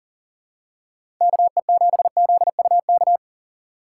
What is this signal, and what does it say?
Morse code, heard as a keyed tone, about 40 wpm, 700 Hz. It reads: KE7ZUK
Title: USA Call Signs - 6x Character Spacing - 40wpm